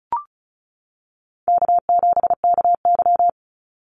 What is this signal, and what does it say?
Morse code, heard as a keyed tone, about 34 wpm, 700 Hz. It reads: K7KY
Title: Call Signs - 35wpm